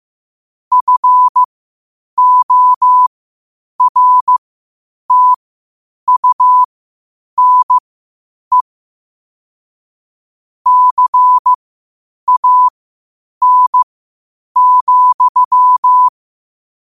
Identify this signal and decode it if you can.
Morse code, heard as a keyed tone, about 15 wpm, spaced as for 8 wpm, 1000 Hz. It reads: FORTUNE CAN,
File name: qotdfiles / 8wpm